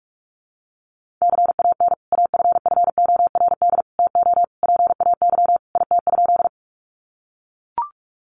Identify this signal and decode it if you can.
Morse code, heard as a keyed tone, about 44 wpm, 700 Hz. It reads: CAN AFFORD TO PAY IT?
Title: Sentences from Top 1100 Words 45wpm